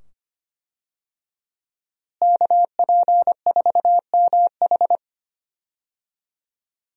Morse code, heard as a keyed tone, about 25 wpm, 700 Hz. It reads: KP4MH
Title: DXCC-100 Call Signs - 5 Characters - Repeated 2x - 25wpm